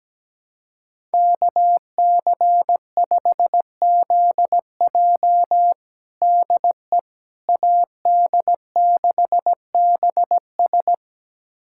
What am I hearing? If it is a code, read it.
Morse code, heard as a keyed tone, about 17 wpm, 700 Hz. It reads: KC5ZJ DE AD6BS
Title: USA Callsign DE Callsign - 5 Characters - 17wpm